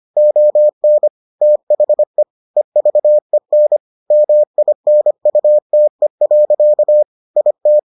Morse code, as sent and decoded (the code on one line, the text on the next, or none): --- -. / - .... / . ...- . -. / -- .. -. ..- - . .-.-.- / .. -
ON THE EVEN MINUTE. IT